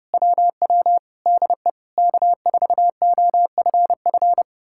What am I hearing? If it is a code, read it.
Morse code, heard as a keyed tone, about 33 wpm, 700 Hz. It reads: WW DE K4OFF